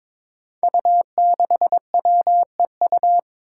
Morse code, heard as a keyed tone, about 22 wpm, 700 Hz. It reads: U6WEU
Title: Five Challenging Letter-Number Combinations 22wpm